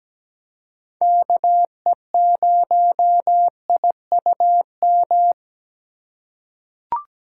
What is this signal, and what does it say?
Morse code, heard as a keyed tone, about 17 wpm, 700 Hz. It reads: KE0IUM
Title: USA Call Signs - 6 Characters - 2x Character Spacing - 17wpm